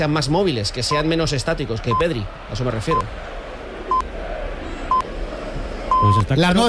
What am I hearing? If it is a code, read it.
Radio time signal (time pips), with speech. It is heard from 0.9 to 6.2 s.